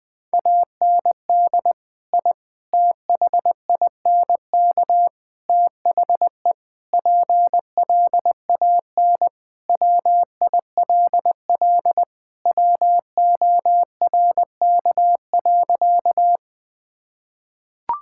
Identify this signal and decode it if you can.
Morse code, heard as a keyed tone, about 20 wpm, 700 Hz. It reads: AND I THINK THE PLAN WILL WORK.
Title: Sentences with 4-Word Phrases - 20wpm